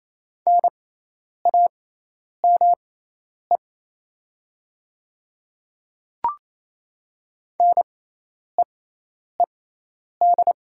Morse code, heard as a keyed tone, about 27 wpm, spaced as for 10 wpm, 700 Hz. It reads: NAME NEED
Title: 6x Character Spacing - Top 200 Words - 28wpm